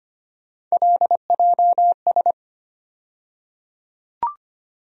Morse code, heard as a keyed tone, about 25 wpm, 700 Hz. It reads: LJS
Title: USA Call Sign Postfixes - 3 Letters 25wpm